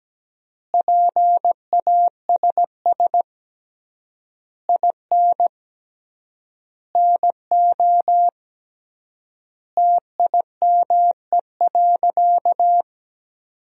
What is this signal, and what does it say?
Morse code, heard as a keyed tone, about 17 wpm, 700 Hz. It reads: PASS IN NO TIME.
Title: 3x Word-Spacing - Sentences from Top 400 Words - 17wpm